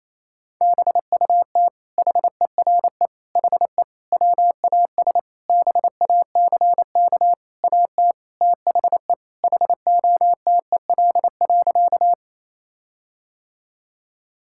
Morse code, heard as a keyed tone, about 28 wpm, 700 Hz. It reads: BUT HERE HE WAS BACK AT THE HOTEL.
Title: Sentences from Top 1600 Words 28wpm